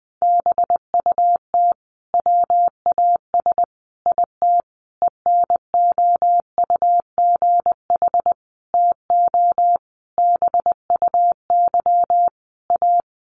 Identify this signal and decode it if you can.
Morse code, heard as a keyed tone, about 20 wpm, 700 Hz. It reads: BUT WAS IT ENOUGH TO BUY A
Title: Sentences from Top 1400 Words 20wpm